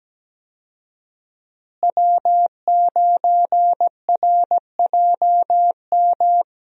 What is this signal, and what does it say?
Morse code, heard as a keyed tone, about 17 wpm, 700 Hz. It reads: W9RJM
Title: USA Call Signs - 5 Characters - Repeated 2x - 17wpm